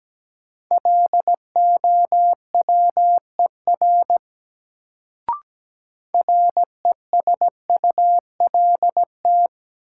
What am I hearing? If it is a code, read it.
Morse code, heard as a keyed tone, about 17 wpm, 700 Hz. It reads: LOWER RESULT
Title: Sentences from Top 700 Words - New Word Drill - 17wpm